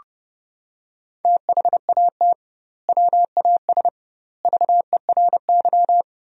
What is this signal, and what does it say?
Morse code, heard as a keyed tone, about 30 wpm, 700 Hz. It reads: THAT WAS VERY
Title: 2x Word-Spacing - Sentences from Top 200 Words - 30wpm